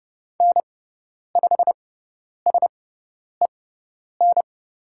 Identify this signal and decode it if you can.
Morse code, heard as a keyed tone, about 30 wpm, spaced as for 10 wpm, 700 Hz. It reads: N5SEN
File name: Lesson 18g - Letter m - Call Signs 30wpm